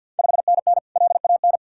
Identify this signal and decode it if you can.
Morse code, heard as a keyed tone, about 47 wpm, 700 Hz. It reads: 5NN LAN
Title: Ontario QSO Party Exchange 50wpm